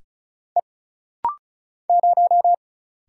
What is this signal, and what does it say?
Morse code, heard as a keyed tone, about 36 wpm, 700 Hz. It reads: E 0
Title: Single Letter-Number - Rapid-Fire 35wpm